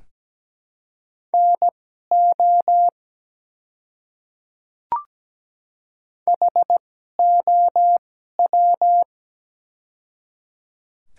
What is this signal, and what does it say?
Morse code, heard as a keyed tone, about 17 wpm, spaced as for 12 wpm, 700 Hz. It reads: NO HOW
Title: 2x Character Spacing - Top 100 Words - 17wpm